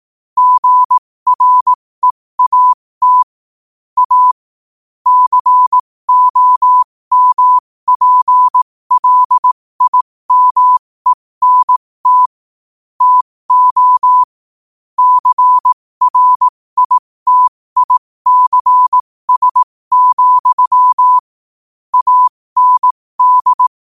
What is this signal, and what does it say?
Morse code, heard as a keyed tone, about 18 wpm, spaced as for 15 wpm, 1000 Hz. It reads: GREAT A COMPLIMENT TO CRITICS, AND
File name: qotdfiles / 15wpm